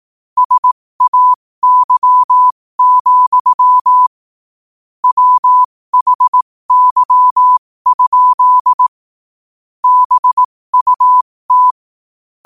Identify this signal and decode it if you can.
Morse code, heard as a keyed tone, about 18 wpm, spaced as for 15 wpm, 1000 Hz. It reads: SAY, WHY? BUT
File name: qotdfiles / 15wpm